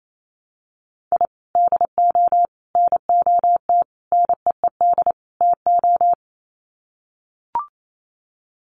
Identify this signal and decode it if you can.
Morse code, heard as a keyed tone, about 28 wpm, 700 Hz. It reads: I DO NOT NEED TO
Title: Sets of 5 Words 28wpm